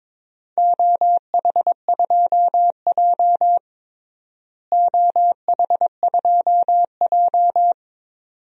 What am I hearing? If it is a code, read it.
Morse code, heard as a keyed tone, about 21 wpm, 700 Hz. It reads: OH2J OH2J
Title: DXCC-100 Call Signs - 4 Characters - Repeated 2x - 22wpm